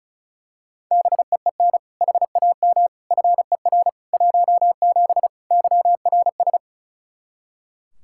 Morse code, heard as a keyed tone, about 36 wpm, 700 Hz. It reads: BEEN HAM FER 17 YRS
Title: Practice QSO - Exchange 3 - More Info - 35wpm